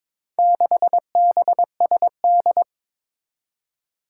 Morse code, heard as a keyed tone, about 22 wpm, 700 Hz. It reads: 6BSD
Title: Four Challenging Letter-Number Combinations 22wpm